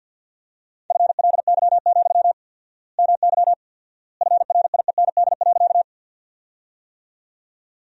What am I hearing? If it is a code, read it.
Morse code, heard as a keyed tone, about 53 wpm, 700 Hz. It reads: FLY, MY FRIEND.
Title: 4x Word-Spacing - Sentences from Top 500 Words - 50wpm